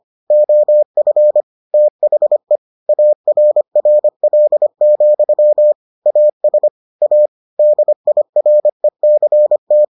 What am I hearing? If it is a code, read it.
Morse code, heard as a keyed tone, about 25 wpm, 600 Hz. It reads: OF THE ARRL, AS A DIRECT